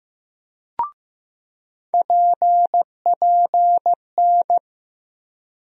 Morse code, heard as a keyed tone, about 15 wpm, 700 Hz. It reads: PPN
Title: USA Call Sign Postfixes - 3 Letters 15wpm